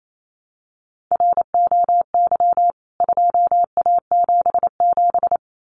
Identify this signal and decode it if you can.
Morse code, heard as a keyed tone, about 28 wpm, 700 Hz. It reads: ROY 2A77
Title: CWT Exchange 28wpm